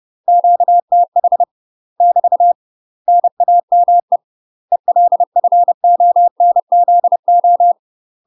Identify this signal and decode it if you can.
Morse code, heard as a keyed tone, about 30 wpm, 700 Hz. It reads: QTH <BT> NAME ELFONZO
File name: UK MASTER QSOs TRAINING / 30WPM / UK MASTER QSOs 0041